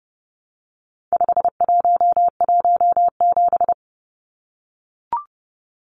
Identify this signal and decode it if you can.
Morse code, heard as a keyed tone, about 30 wpm, 700 Hz. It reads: 5117
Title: Numbers - 4-Digit 30wpm